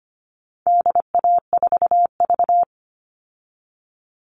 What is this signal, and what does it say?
Morse code, heard as a keyed tone, about 25 wpm, 700 Hz. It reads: DA4V
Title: All Cut Numbers - 4-Digit 25wpm